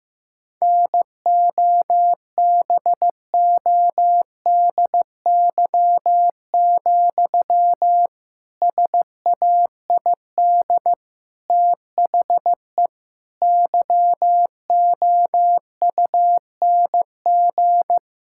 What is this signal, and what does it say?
Morse code, heard as a keyed tone, about 16 wpm, 700 Hz. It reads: NOBODY, SAID THE YOUNG
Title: Sentences from Top 1600 Words 15wpm